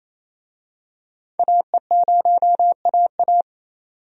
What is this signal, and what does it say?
Morse code, heard as a keyed tone, about 28 wpm, 700 Hz. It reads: AE0AA
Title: USA Call Signs - 5 Characters - 28wpm